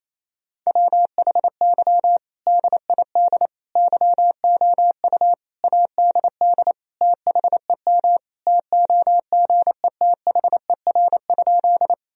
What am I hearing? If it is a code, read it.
Morse code, heard as a keyed tone, about 27 wpm, 700 Hz. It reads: WHY DID YOU ADD THEM TOGETHER?
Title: Sentences from Top 1000 Words - Review Part 1 - 28wpm